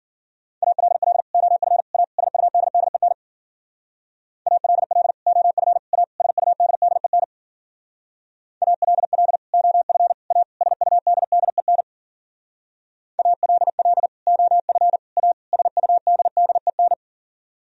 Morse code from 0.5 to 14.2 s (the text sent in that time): ALL OF A SUDDEN ALL OF A SUDDEN ALL OF A SUDDEN ALL